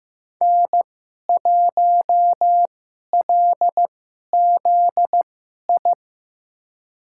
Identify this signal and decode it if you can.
Morse code, heard as a keyed tone, about 15 wpm, spaced as for 11 wpm, 700 Hz. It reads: N1LZI
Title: USA Call Signs - 5 Characters - 2x Character Spacing - 15wpm